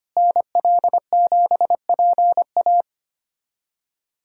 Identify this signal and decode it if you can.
Morse code, heard as a keyed tone, about 25 wpm, 700 Hz. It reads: NL7PA
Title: USA Callsign DE Callsign - 5 Characters - 25wpm